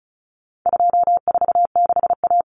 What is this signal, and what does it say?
Morse code, heard as a keyed tone, about 34 wpm, 700 Hz. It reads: 246A
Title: Common Cut Numbers - 4-Digit 35wpm